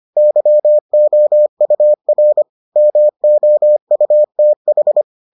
Morse code, heard as a keyed tone, about 25 wpm, 600 Hz. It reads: YOUR MOUTH